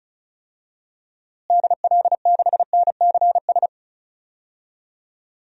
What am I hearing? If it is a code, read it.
Morse code, heard as a keyed tone, about 36 wpm, 700 Hz. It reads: DL6NCS
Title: DXCC-100 Call Signs - 6 Characters - Repeated 2x - 35wpm